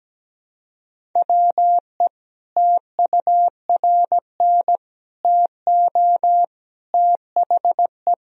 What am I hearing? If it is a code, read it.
Morse code, heard as a keyed tone, about 17 wpm, 700 Hz. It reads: WE TURN TO THE
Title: Sentences from Top 1000 Words - Review Part 1 - 17wpm